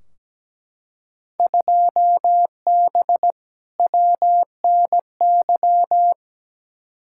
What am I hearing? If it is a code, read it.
Morse code, heard as a keyed tone, about 17 wpm, 700 Hz. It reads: 2B WNY
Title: Field Day Exchanges Encore - 17wpm